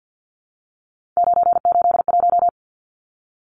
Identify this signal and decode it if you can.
Morse code, heard as a keyed tone, about 48 wpm, 700 Hz. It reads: 981
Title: Numbers - 3-Digit 50wpm